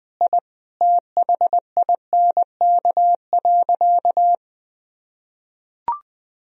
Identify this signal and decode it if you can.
Morse code, heard as a keyed tone, about 20 wpm, 700 Hz. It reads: I THINK.
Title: Sentences from Top 1000 Words - Review Part 2 - 20wpm